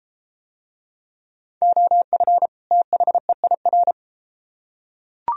Morse code, heard as a keyed tone, about 33 wpm, 700 Hz. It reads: OF THEIR